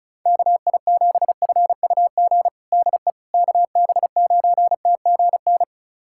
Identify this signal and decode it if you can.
Morse code, heard as a keyed tone, about 34 wpm, 700 Hz. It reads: KI7FUG DE KB9TGN
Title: USA Callsign DE Callsign - 6 Characters - 35wpm